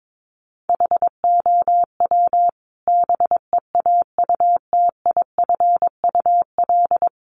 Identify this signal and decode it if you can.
Morse code, heard as a keyed tone, about 22 wpm, 700 Hz. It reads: HOW BEAUTIFUL
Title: Sentences from Top 900 Words 22wpm